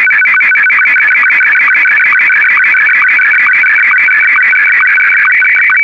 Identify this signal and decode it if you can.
SSTV (slow-scan television), Martin 1.
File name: sstv-analog-m1